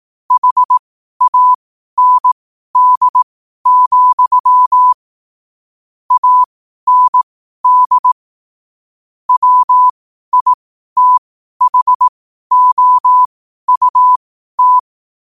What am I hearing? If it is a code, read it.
Morse code, heard as a keyed tone, about 18 wpm, spaced as for 12 wpm, 1000 Hz. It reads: HAND, AND WITHOUT
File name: qotdfiles / 12wpm